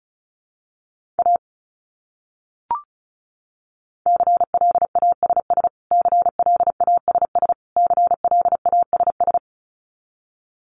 Morse code, heard as a keyed tone, about 34 wpm, 700 Hz. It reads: A CLASS CLASS CLASS
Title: Top 500 Words Repeated 35wpm